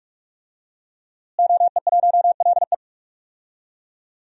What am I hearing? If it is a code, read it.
Morse code, heard as a keyed tone, about 46 wpm, 700 Hz. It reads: OE1LE
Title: DXCC-100 Call Signs - 5 Characters - Repeated 3x - 45wpm